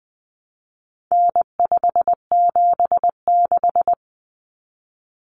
Morse code, heard as a keyed tone, about 20 wpm, 700 Hz. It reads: N576
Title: Common Cut Numbers - 4-Digit 20wpm